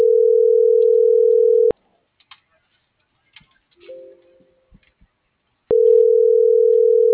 Ringback tone (North America), 440 + 480 Hz.